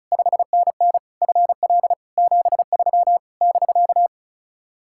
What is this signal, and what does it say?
Morse code, heard as a keyed tone, about 46 wpm, 700 Hz. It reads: 5NN FL 73 <BK>